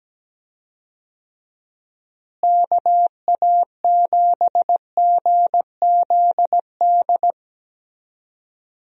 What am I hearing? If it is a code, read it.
Morse code, heard as a keyed tone, about 17 wpm, 700 Hz. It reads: KA7GZD
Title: USA Call Signs - 6 Characters - Repeated 2x - 17wpm